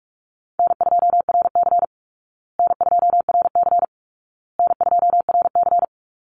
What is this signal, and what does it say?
Morse code, heard as a keyed tone, about 45 wpm, 700 Hz. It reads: N2RC N2RC N2RC
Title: Easy Call Signs Repeated 3x - 45wpm